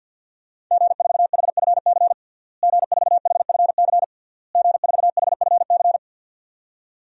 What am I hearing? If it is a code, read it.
Morse code, heard as a keyed tone, about 51 wpm, 700 Hz. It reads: G4HFC G4HFC G4HFC
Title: DXCC-100 Call Signs - 5 Characters - Repeated 3x - 50wpm